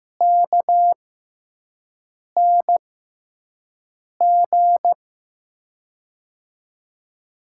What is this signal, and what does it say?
Morse code, heard as a keyed tone, about 15 wpm, spaced as for 5 wpm, 700 Hz. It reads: KNG
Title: USA Call Signs - 6x Character Spacing - 15wpm